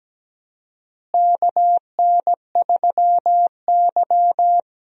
Morse code, heard as a keyed tone, about 17 wpm, 700 Hz. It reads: KN3Y
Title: USA Call Signs - 4 Characters - Repeated 3x - 17wpm